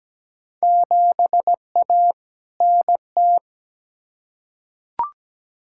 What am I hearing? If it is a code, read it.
Morse code, heard as a keyed tone, about 17 wpm, 700 Hz. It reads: 7A NT
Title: Field Day Exchanges Encore - 17wpm